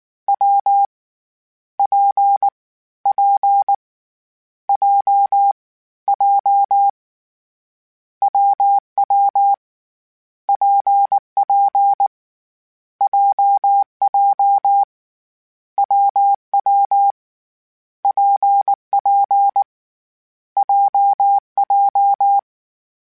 Morse code, heard as a keyed tone, about 19 wpm, 800 Hz. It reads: W P P J J WW PP JJ WW PP JJ
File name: Difficult Combos / Difficult Combos-mp3s / WPJ WPJ1 @18